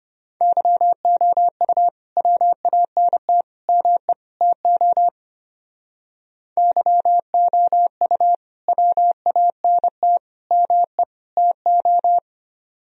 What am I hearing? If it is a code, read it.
Morse code, heard as a keyed tone, about 32 wpm, 700 Hz. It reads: YOU WANT ME TO YOU WANT ME TO